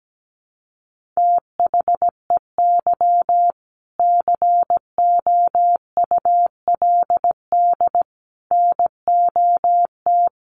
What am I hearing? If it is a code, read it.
Morse code, heard as a keyed tone, about 17 wpm, 700 Hz. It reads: THEY COULD NOT
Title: Sentences from Top 900 Words 17wpm